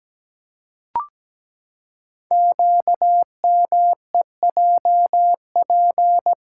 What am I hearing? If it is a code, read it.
Morse code, heard as a keyed tone, about 17 wpm, 700 Hz. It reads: QMEJP
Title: Code Groups - 5 Letter:Numbers - 17wpm